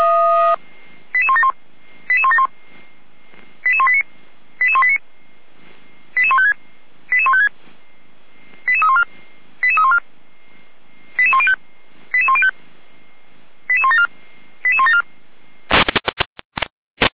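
Five-tone selective call (ZVEI), heard as quick runs of five tones, tones of about 80 ms. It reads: 88171 88171 88189 88189 88156 88156 88314 88314 88105 88105 88174 88174
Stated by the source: ZVEI five-tone call received on SDR.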